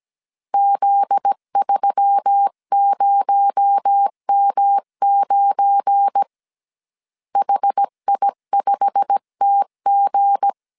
Morse code, heard as a keyed tone, about 17 wpm, 800 Hz. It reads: Z30M9 HI5TG
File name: CW / DET / 1 9 1500C S B 16wpm 2856